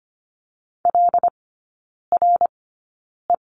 Morse code, heard as a keyed tone, about 25 wpm, spaced as for 9 wpm, 700 Hz. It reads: LRE
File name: Lesson 14e - Number 2 - Three Characters 25wpm